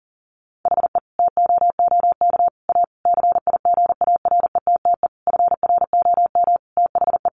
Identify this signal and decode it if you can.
Morse code, heard as a keyed tone, about 40 wpm, 700 Hz. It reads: HE TOOK A CIGARETTE FROM THE
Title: Sentences from Top 1000 Words 40wpm